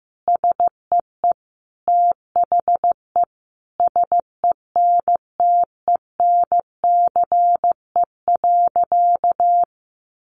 Morse code, heard as a keyed tone, about 15 wpm, 700 Hz. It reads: SEE THE SENTENCE.
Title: Additional Easy Phrases 15wpm